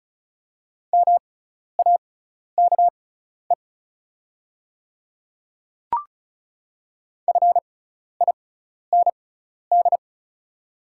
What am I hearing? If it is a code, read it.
Morse code, heard as a keyed tone, about 36 wpm, spaced as for 12 wpm, 700 Hz. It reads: MAKE FIND
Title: 6x Character Spacing - Top 100 Words - 35wpm